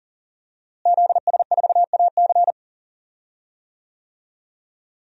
Morse code, heard as a keyed tone, about 40 wpm, 700 Hz. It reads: ZS4AC
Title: DXCC-100 Call Signs - 5 Characters - Repeated 2x - 40wpm